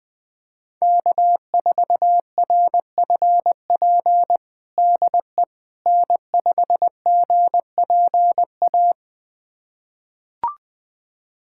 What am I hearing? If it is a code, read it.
Morse code, heard as a keyed tone, about 20 wpm, 700 Hz. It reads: K4RFP DE N5GPA
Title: USA Callsign DE Callsign - 5 Characters - 20wpm